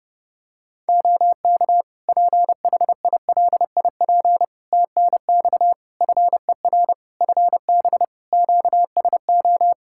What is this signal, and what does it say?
Morse code, heard as a keyed tone, about 30 wpm, 700 Hz. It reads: OK PHILIP TNX FER FB QSO
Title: K1USN SST QSO Practice 30wpm